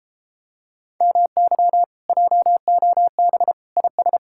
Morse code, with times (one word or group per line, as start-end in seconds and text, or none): 1.0-1.8 MY
2.1-3.5 JOB
3.8-4.2 IS